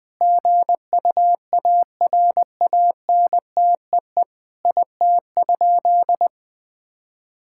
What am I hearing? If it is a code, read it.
Morse code, heard as a keyed tone, about 21 wpm, 700 Hz. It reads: GUARANTEE IT?
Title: Sentences from Top 1600 Words 20wpm